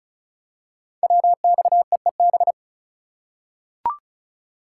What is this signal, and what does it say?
Morse code, heard as a keyed tone, about 36 wpm, 690 Hz. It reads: WXEEB
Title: Code Groups - 5 Letters - 35wpm